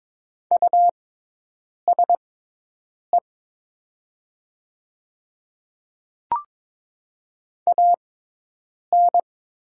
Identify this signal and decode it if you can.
Morse code, heard as a keyed tone, about 22 wpm, spaced as for 8 wpm, 700 Hz. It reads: USE AN
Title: 6x Character Spacing - Top 300 Words - 22wpm